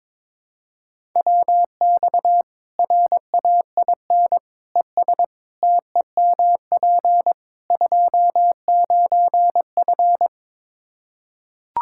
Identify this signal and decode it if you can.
Morse code, heard as a keyed tone, about 21 wpm, 700 Hz. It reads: WX RAIN ES TEMP 29F
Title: Practice QSO - Exchange 2 - Info - 22wpm